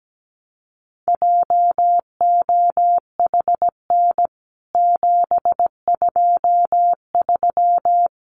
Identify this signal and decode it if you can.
Morse code, heard as a keyed tone, about 17 wpm, 700 Hz. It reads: JOHN 723
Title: CWT Exchange 17wpm